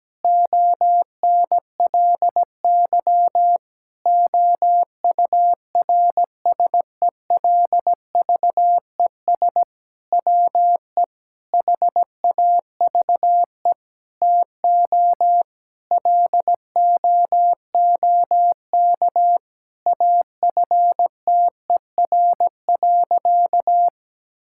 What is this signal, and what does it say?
Morse code, heard as a keyed tone, about 17 wpm, 700 Hz. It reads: ONLY OURSELVES WE HAVE TO LOOK AFTER.
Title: Sentences from Top 1600 Words 17wpm